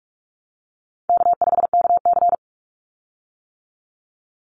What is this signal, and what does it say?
Morse code, heard as a keyed tone, about 44 wpm, 700 Hz. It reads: K5KC
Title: Call Signs - 45wpm